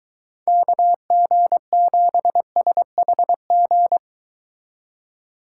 Morse code, heard as a keyed tone, about 22 wpm, 700 Hz. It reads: KG7SHG